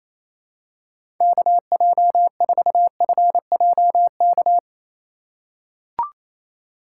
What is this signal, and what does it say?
Morse code, heard as a keyed tone, about 27 wpm, 700 Hz. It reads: KJ4FJK
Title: USA Call Signs - 4x Character Spacing - 28wpm